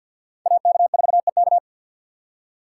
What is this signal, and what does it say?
Morse code, heard as a keyed tone, about 52 wpm, 700 Hz. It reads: AK4EK
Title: USA Call Signs - 5 Characters - 3x Character Spacing - 50wpm